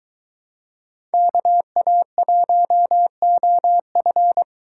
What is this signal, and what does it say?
Morse code, heard as a keyed tone, about 22 wpm, 700 Hz. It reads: KA1OF